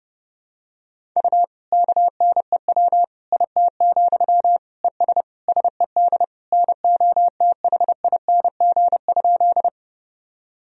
Morse code, heard as a keyed tone, about 30 wpm, 700 Hz. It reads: U KNEW IT, ES SED NOTHING?
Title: Sentences from Top 500 Words and Abbreviations 30wpm